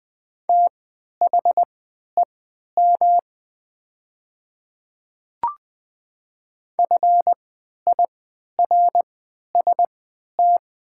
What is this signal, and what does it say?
Morse code, heard as a keyed tone, about 20 wpm, spaced as for 11 wpm, 700 Hz. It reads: THEM FIRST
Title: 3x Character Spacing - Top 100 Words - 20wpm